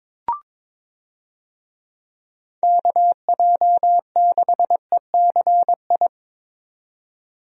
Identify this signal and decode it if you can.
Morse code, heard as a keyed tone, about 22 wpm, 700 Hz. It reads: KJ6ECI
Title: USA Call Signs - 6 Characters - 22wpm